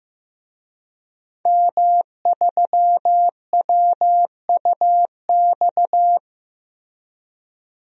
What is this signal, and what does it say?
Morse code, heard as a keyed tone, about 15 wpm, 700 Hz. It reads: M3WUX
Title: DXCC-100 Call Signs - 5 Characters - Repeated 3x - 15wpm